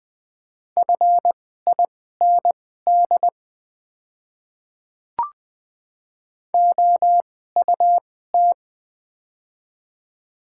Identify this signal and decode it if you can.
Morse code, heard as a keyed tone, about 20 wpm, spaced as for 14 wpm, 700 Hz. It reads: FIND OUT
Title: 2x Character Spacing - Top 100 Words - 20wpm